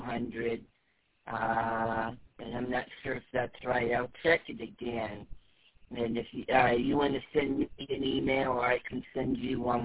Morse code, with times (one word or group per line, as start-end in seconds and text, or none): none